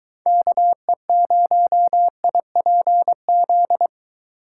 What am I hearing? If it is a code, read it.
Morse code, heard as a keyed tone, about 23 wpm, 700 Hz. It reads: KE0IPZ